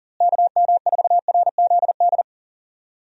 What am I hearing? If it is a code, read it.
Morse code, heard as a keyed tone, about 40 wpm, 700 Hz. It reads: KM4RZD
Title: USA Call Signs - 6 Characters - Repeated 2x - 40wpm